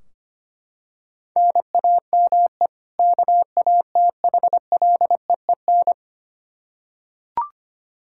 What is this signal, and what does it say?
Morse code, heard as a keyed tone, about 25 wpm, 700 Hz. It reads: NAME KATHLEEN
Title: Practice QSO - Exchange 1 - Beginning - 25wpm